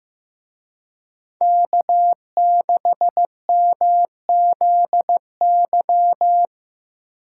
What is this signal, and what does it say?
Morse code, heard as a keyed tone, about 17 wpm, 700 Hz. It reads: K6MZY
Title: Speed Racing - USA 1x3 Call Signs 15wpm